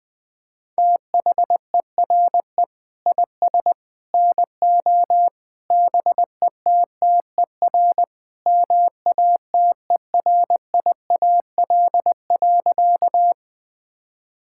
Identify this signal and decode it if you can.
Morse code, heard as a keyed tone, about 20 wpm, 700 Hz. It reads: THERE IS NO BETTER MATERIAL.
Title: Sentences from Top 1000 Words - Review Part 5 - 20wpm